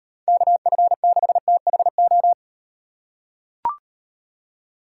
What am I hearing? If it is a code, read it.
Morse code, heard as a keyed tone, about 36 wpm, 700 Hz. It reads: KF6THO